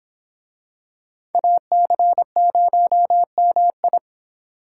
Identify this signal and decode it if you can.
Morse code, heard as a keyed tone, about 25 wpm, 700 Hz. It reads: AC0MI